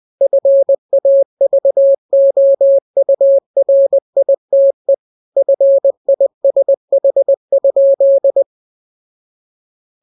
Morse code, heard as a keyed tone, about 20 wpm, 550 Hz. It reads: FAVOURITE FISH?
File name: jokes / joke 6 20 0000